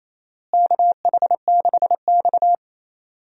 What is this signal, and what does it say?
Morse code, heard as a keyed tone, about 27 wpm, 700 Hz. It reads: KH6X
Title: DXCC-100 Call Signs - 4 Characters - Repeated 2x - 28wpm